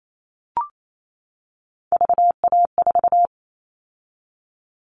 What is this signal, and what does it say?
Morse code, heard as a keyed tone, about 28 wpm, 700 Hz. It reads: VA4
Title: All Cut Numbers - 3-Digit 28wpm